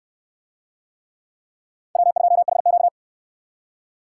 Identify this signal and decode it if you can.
Morse code, heard as a keyed tone, about 65 wpm, 700 Hz. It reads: W2HQ